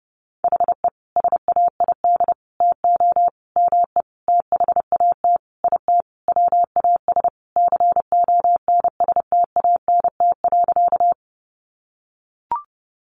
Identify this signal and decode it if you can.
Morse code, heard as a keyed tone, about 30 wpm, 700 Hz. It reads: HE SAID TO ME THAT IT WAS CONSTANT.
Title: Sentences from Top 1500 Words 30wpm